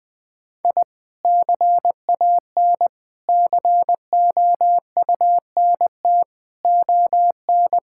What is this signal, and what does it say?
Morse code, heard as a keyed tone, about 20 wpm, 700 Hz. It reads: I CAN COUNT ON
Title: Sentences with 8-Word Phrases - 20wpm